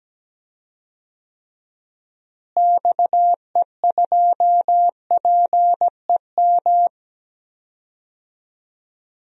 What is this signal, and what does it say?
Morse code, heard as a keyed tone, about 17 wpm, 700 Hz. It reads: XE2PEM
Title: DXCC-100 Call Signs - 6 Characters - Repeated 3x - 17wpm